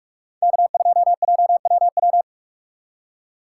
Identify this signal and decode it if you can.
Morse code, heard as a keyed tone, about 42 wpm, 700 Hz. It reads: K2JWW
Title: USA Call Signs - 5 Characters - 3x Character Spacing - 45wpm